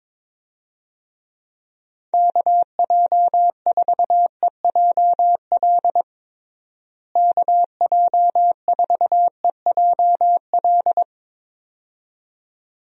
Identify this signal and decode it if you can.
Morse code, heard as a keyed tone, about 22 wpm, 700 Hz. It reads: KJ4EJL KJ4EJL
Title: USA Call Signs - 6 Characters - Repeated 2x - 22wpm